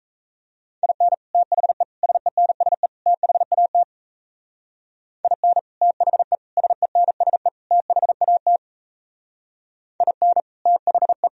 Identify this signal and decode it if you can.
Morse code, heard as a keyed tone, about 35 wpm, 700 Hz. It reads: IN THE SENSE THAT IN THE SENSE THAT IN THE